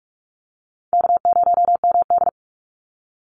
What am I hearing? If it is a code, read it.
Morse code, heard as a keyed tone, about 44 wpm, 700 Hz. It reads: K0MD
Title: Call Signs Repeated 2x - 45wpm